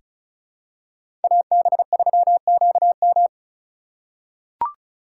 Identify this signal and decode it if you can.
Morse code, heard as a keyed tone, about 34 wpm, 700 Hz. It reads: AB3QM
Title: USA Call Signs - 5 Characters - 2x Character Spacing - 35wpm